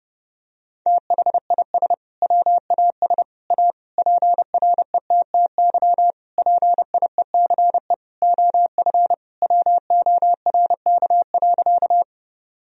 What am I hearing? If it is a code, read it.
Morse code, heard as a keyed tone, about 30 wpm, 700 Hz. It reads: THIS WAS A PRETTY PIECE OF WORK.
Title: Sentences from Top 1000 Words - Review Part 2 - 30wpm